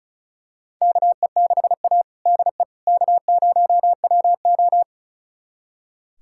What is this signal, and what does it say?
Morse code, heard as a keyed tone, about 34 wpm, 700 Hz. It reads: KE6A DE K0WO
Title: USA Callsign DE Callsign - 4 Characters - 35wpm